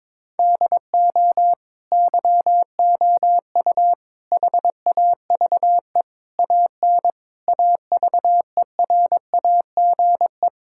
Morse code, heard as a keyed tone, about 22 wpm, 700 Hz. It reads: DO YOU HAVE AN AVERAGE
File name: Sets of 9 Words - 22wpm